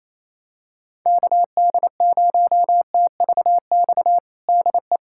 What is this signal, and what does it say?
Morse code, heard as a keyed tone, about 27 wpm, 700 Hz. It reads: KD0TVX DE
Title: USA Callsign DE Callsign 28wpm